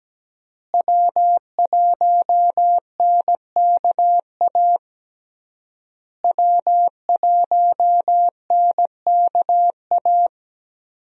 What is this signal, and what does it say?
Morse code, heard as a keyed tone, about 17 wpm, 700 Hz. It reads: W1NKA W1NKA
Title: USA Call Signs - 5 Characters - Repeated 3x - 17wpm